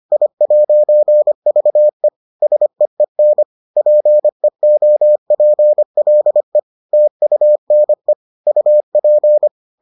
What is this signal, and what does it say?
Morse code, heard as a keyed tone, about 25 wpm, 600 Hz. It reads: I'VE SEEN PEOPLE TUNE UP